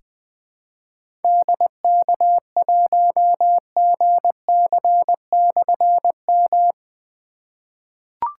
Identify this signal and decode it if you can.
Morse code, heard as a keyed tone, about 20 wpm, 700 Hz. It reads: DK1GC/M
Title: Difficult Call Signs 20wpm